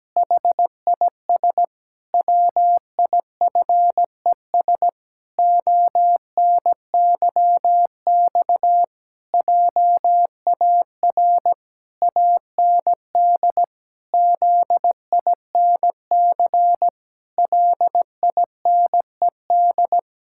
Morse code, heard as a keyed tone, about 17 wpm, 700 Hz. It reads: HIS WIFES ONYX JAR AND ZINC LINED
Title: Pangram Challenge v2 - 17wpm